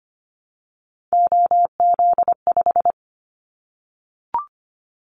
Morse code, heard as a keyed tone, about 25 wpm, 700 Hz. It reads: OZ5
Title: Callsign Prefixes for Top 100 DXCC - 25wpm